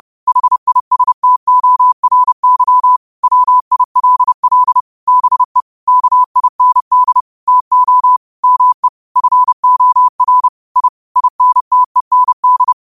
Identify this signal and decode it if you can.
Morse code, heard as a keyed tone, about 30 wpm, 1000 Hz. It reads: HISTORY WILL BE KIND TO ME FOR I INTEND
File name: qotdfiles / 30wpm